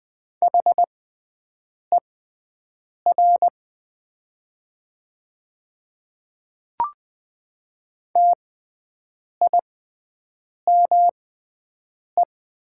Morse code, heard as a keyed tone, about 20 wpm, spaced as for 7 wpm, 700 Hz. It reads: HER TIME
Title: 6x Character Spacing - Top 200 Words - 20wpm